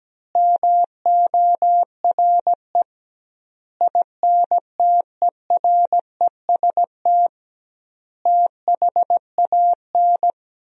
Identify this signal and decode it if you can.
Morse code, heard as a keyed tone, about 17 wpm, 700 Hz. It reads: MORE INTEREST THAN
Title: 2x Word-Spacing - Sentences from Top 400 Words - 17wpm